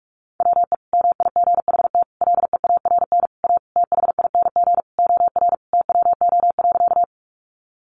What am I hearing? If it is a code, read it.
Morse code, heard as a keyed tone, about 44 wpm, 700 Hz. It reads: WE MIGHT LEARN A THING OR TWO.
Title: Sentences from Top 300 Words 45wpm